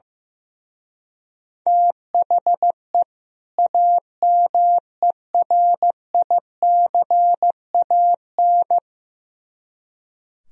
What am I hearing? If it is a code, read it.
Morse code, heard as a keyed tone, about 16 wpm, 700 Hz. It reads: THE AMERICAN
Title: Speed-Racing - Sets of 2 Words 15wpm